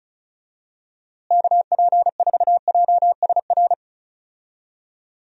Morse code, heard as a keyed tone, about 36 wpm, 700 Hz. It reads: KP4JSR
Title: USA Call Signs - 6 Characters - Repeated 2x - 35wpm